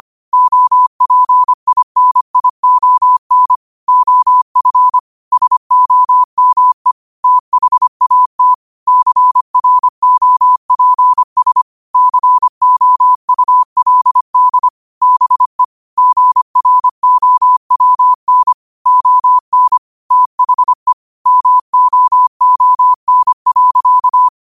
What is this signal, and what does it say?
Morse code, heard as a keyed tone, about 25 wpm, 1000 Hz. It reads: OPINION OF SOME THAT CROPS COULD BE GROWN ON THE MOON.
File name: qotdfiles / 25wpm